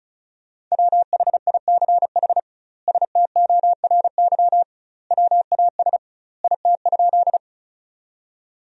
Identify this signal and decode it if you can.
Morse code, heard as a keyed tone, about 36 wpm, 700 Hz. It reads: WHICH STORY WAS IT?
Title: 2x Word-Spacing - Sentences from Top 300 Words - 35wpm